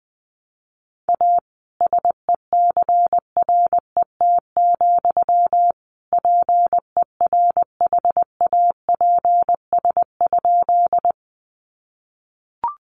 Morse code, heard as a keyed tone, about 20 wpm, 700 Hz. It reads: A SECRET, PERHAPS?
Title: Sentences from Top 1200 Words 20wpm